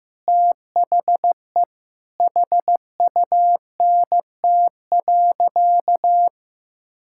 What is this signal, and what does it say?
Morse code, heard as a keyed tone, about 15 wpm, 700 Hz. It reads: THE HUNT.
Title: Sentences from Top 1600 Words 15wpm